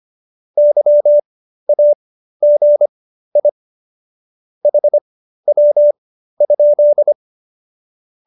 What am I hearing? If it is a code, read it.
Morse code, heard as a keyed tone, about 25 wpm, spaced as for 13 wpm, 590 Hz. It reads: YAGI HW?